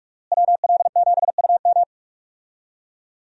Morse code, heard as a keyed tone, about 46 wpm, 700 Hz. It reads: WL7UM
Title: USA Call Signs - 5 Characters - Repeated 3x - 45wpm